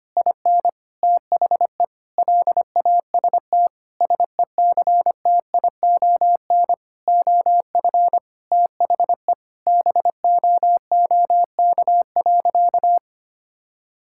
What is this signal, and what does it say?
Morse code, heard as a keyed tone, about 25 wpm, 700 Hz. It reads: IN THE LAST SECTION OF THE BOOK.
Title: Sentences from Top 1000 Words - Review Part 5 - 25wpm